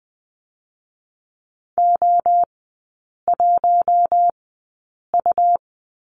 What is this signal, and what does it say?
Morse code, heard as a keyed tone, about 20 wpm, spaced as for 8 wpm, 700 Hz. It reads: O1U
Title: 114 Lesson 19e - Letter w - Three Characters 20wpm